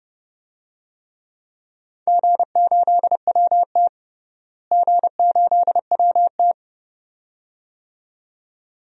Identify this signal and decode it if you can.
Morse code, heard as a keyed tone, about 30 wpm, 700 Hz. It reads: G8WT G8WT
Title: DXCC-100 Call Signs - 4 Characters - Repeated 2x - 30wpm